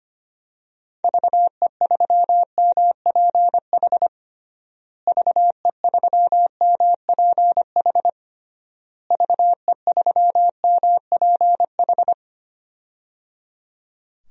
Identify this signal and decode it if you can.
Morse code, heard as a keyed tone, about 25 wpm, 700 Hz. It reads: VE3MPH VE3MPH VE3MPH
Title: DXCC-100 Call Signs - 6 Characters - Repeated 3x - 25wpm